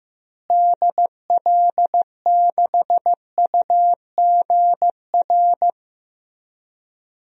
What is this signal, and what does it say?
Morse code, heard as a keyed tone, about 15 wpm, 700 Hz. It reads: DL6UGR
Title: DXCC-100 Call Signs - 6 Characters - Repeated 3x - 15wpm